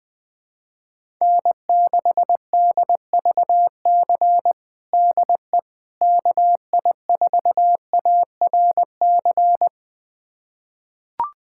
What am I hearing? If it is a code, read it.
Morse code, heard as a keyed tone, about 20 wpm, 700 Hz. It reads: N6DVC DE KI4ARC
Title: USA Callsign DE Callsign 20wpm